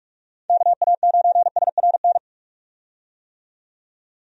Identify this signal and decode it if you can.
Morse code, heard as a keyed tone, about 46 wpm, 700 Hz. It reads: KA9SRN
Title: USA Call Signs - 6 Characters - 45wpm